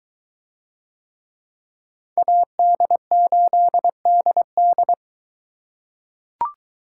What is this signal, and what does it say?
Morse code, heard as a keyed tone, about 21 wpm, 700 Hz. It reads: AD8DD